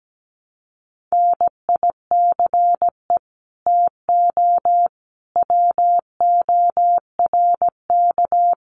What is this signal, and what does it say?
Morse code, heard as a keyed tone, about 17 wpm, 700 Hz. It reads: NICE TO WORK
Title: Common QSO Phrases 17wpm